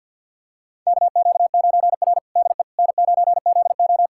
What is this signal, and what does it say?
Morse code, heard as a keyed tone, about 46 wpm, 700 Hz. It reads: KQ9R DE N9ZO
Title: USA Callsign DE Callsign - 4 Characters - 50wpm